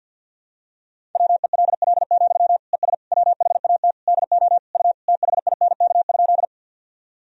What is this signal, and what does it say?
Morse code, heard as a keyed tone, about 51 wpm, 700 Hz. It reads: WELL, ES WHAT DO U THINK?
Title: Sentences from Top 200 Words and Abbreviations 50wpm